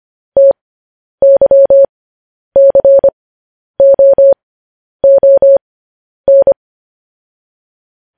Morse code, heard as a keyed tone, about 25 wpm, spaced as for 10 wpm, 550 Hz. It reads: TYCOON
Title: News Headlines at 10